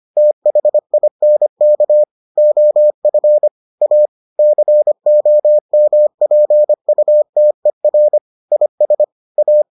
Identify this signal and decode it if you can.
Morse code, heard as a keyed tone, about 25 wpm, 600 Hz. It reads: THINK OF A COMPUTER IS A